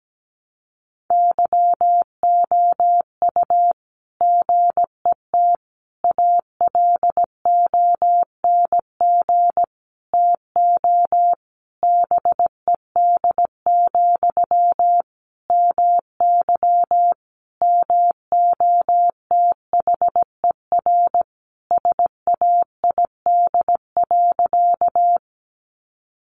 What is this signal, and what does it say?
Morse code, heard as a keyed tone, about 17 wpm, 700 Hz. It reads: YOU GET ALONG TO BED, MY MOTHER SAID.
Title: Sentences from Top 700 Words 17wpm